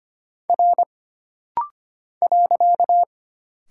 Morse code, heard as a keyed tone, about 25 wpm, 700 Hz. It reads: R .
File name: Single Character (Letter:Number:Punctuation:Prosign) - Rapid-Fire - 25wpm